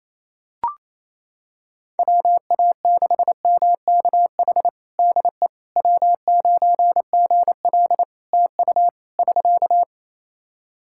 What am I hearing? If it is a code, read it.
Morse code, heard as a keyed tone, about 28 wpm, 700 Hz. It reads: WA6MKH DE W9GL TU <SK>
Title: Practice QSO - Exchange 4 - Ending - 28wpm